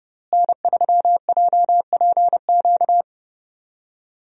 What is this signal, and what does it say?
Morse code, heard as a keyed tone, about 31 wpm, 700 Hz. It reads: N3JPQ
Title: USA Call Signs - 5 Characters - Repeated 2x - 30wpm